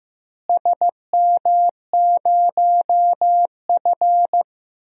Morse code, heard as a keyed tone, about 15 wpm, 700 Hz. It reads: SM0F
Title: DXCC-100 Call Signs - 4 Characters - Repeated 2x - 15wpm